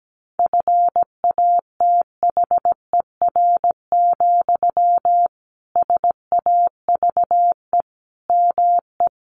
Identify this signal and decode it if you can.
Morse code, heard as a keyed tone, about 17 wpm, 700 Hz. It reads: FATHER, SAVE ME
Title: Sentences from Top 1100 Words 17wpm